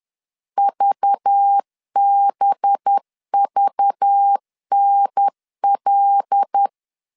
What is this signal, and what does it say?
Morse code, heard as a keyed tone, about 10 wpm, 800 Hz. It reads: VBVNL
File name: LEZIONI / 9 2867 10wpm 2733